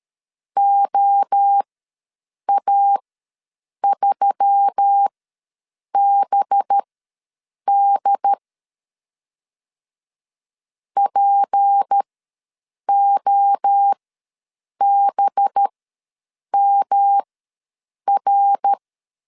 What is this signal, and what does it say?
Morse code, heard as a keyed tone, about 13 wpm, spaced as for 7 wpm, 800 Hz. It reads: OA3BD POBMR
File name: LEZIONI / 6 PKX0 / 6 P ws 12wpm 3117